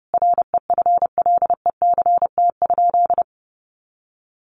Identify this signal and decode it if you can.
Morse code, heard as a keyed tone, about 30 wpm, 700 Hz. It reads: REFLECT?
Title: Sentences from Top 1500 Words 30wpm